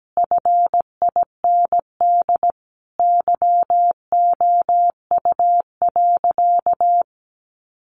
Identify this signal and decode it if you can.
Morse code, heard as a keyed tone, about 17 wpm, 700 Hz. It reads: FIND YOU.
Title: Sentences from Top 1300 Words 17wpm